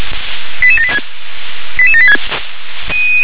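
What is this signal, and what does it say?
Five-tone selective call (ZVEI), heard as quick runs of five tones, tones of about 70 ms. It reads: 88786
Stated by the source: ZVEI five-tone selective call received on SDR.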